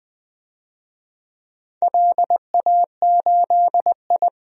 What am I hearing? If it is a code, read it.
Morse code, heard as a keyed tone, about 20 wpm, 700 Hz. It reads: LA8I
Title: DXCC-100 Call Signs - 4 Characters - Repeated 2x - 20wpm